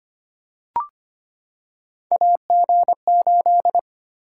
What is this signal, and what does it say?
Morse code, heard as a keyed tone, about 25 wpm, 700 Hz. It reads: AG8
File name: Three Letter-Number Combinations - 25wpm